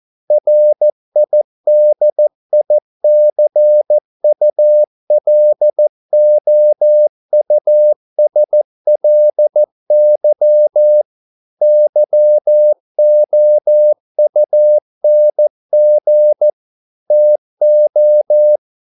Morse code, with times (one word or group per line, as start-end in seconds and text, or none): 0.3-11.0 RIDICULOUSLY
11.6-16.5 YOUNG
17.1-18.6 TO